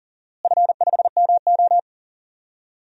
Morse code, heard as a keyed tone, about 38 wpm, 700 Hz. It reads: F5MO
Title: DXCC-100 Call Signs - 4 Characters - Repeated 2x - 40wpm